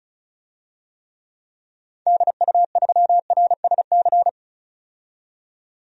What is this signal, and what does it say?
Morse code, heard as a keyed tone, about 36 wpm, 700 Hz. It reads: DU3RSC
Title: DXCC-100 Call Signs - 6 Characters - 35wpm